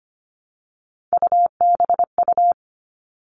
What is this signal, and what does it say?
Morse code, heard as a keyed tone, about 25 wpm, 700 Hz. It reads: UBU
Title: All Cut Numbers - 3-Digit 25wpm